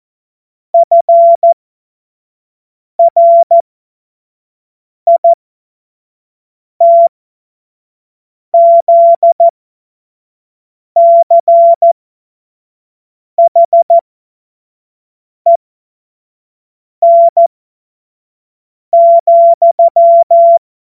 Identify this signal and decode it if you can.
Morse code, heard as a keyed tone, about 14 wpm, spaced as for 5 wpm, 680 Hz. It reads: FRITZCHEN,
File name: Morsekurs / Witze / Witz 27. September 2022 25 F70